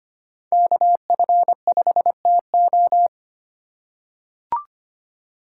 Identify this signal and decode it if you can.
Morse code, heard as a keyed tone, about 25 wpm, 700 Hz. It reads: KF5TO
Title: USA Call Signs - 5 Characters - 2x Character Spacing - 25wpm